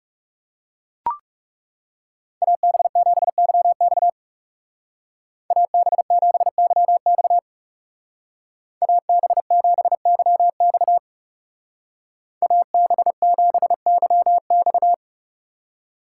Morse code, heard as a keyed tone, about 37 wpm, 700 Hz. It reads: AB7YX AB7YX AB7YX AB7YX
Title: Speed Racing - USA 2x2 Call Signs 30wpm